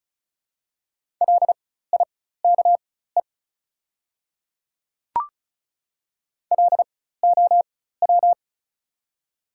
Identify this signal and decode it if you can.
Morse code, heard as a keyed tone, about 34 wpm, spaced as for 16 wpm, 690 Hz. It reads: LIKE LOW
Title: 4x Character Spacing - Top 200 Words - 35wpm